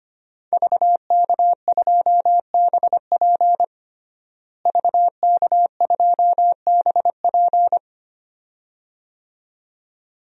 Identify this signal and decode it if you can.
Morse code, heard as a keyed tone, about 25 wpm, 700 Hz. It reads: VK2BP VK2BP
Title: DXCC-100 Call Signs - 5 Characters - Repeated 2x - 25wpm